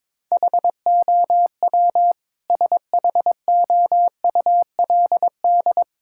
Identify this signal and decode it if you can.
Morse code, heard as a keyed tone, about 22 wpm, 700 Hz. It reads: HOW SHOULD
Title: Sentences from Top 1000 Words - Review Part 6 - 22wpm